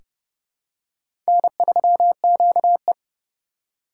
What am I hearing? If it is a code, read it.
Morse code, heard as a keyed tone, about 30 wpm, 700 Hz. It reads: N3QE
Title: K1USN SST QSO Practice - 30wpm